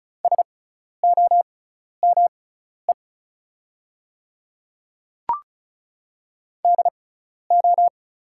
Morse code, heard as a keyed tone, about 36 wpm, spaced as for 12 wpm, 700 Hz. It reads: SOME DO
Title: 6x Character Spacing - Top 200 Words - 35wpm